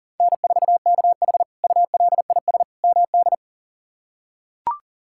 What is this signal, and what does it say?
Morse code, heard as a keyed tone, about 37 wpm, 700 Hz. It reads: N4KH ULIS MD
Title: K1USN SST QSO Practice - 40wpm